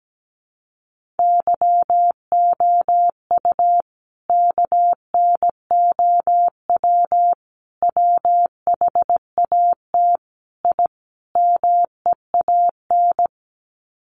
Morse code, heard as a keyed tone, about 17 wpm, 700 Hz. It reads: YOU KNOW WHAT I MEAN
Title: Sets of 5 Words 17wpm